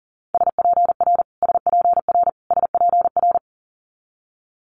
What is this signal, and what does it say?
Morse code, heard as a keyed tone, about 42 wpm, 700 Hz. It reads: SPR SPR SPR
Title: Multi-letter Phonemes Repeated 3x 40wpm